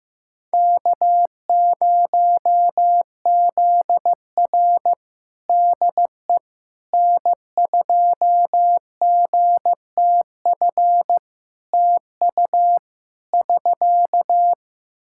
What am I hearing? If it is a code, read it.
Morse code, heard as a keyed tone, about 15 wpm, 700 Hz. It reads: K0ZR DE N2GTF TU <SK>
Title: Practice QSO - Exchange 4 - Ending - 15wpm